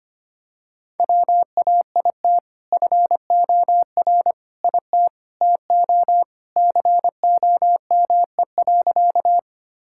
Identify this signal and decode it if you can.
Morse code, heard as a keyed tone, about 25 wpm, 700 Hz. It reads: WAIT FOR IT TO COME.
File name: practice / 25wpm / Sentences with 2-Word Phrases - 25wpm